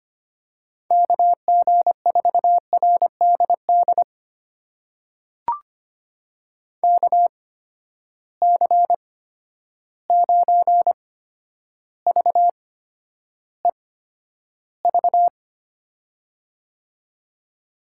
Morse code, heard as a keyed tone, about 25 wpm, spaced as for 11 wpm, 700 Hz. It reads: KG4RDD KC9VEV
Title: USA Call Signs - 6 Characters - 8x Character Spacing - 25wpm